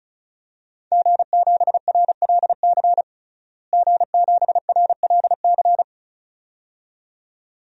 Morse code, heard as a keyed tone, about 36 wpm, 700 Hz. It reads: G7RLC G7RLC
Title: DXCC-100 Call Signs - 5 Characters - Repeated 2x - 35wpm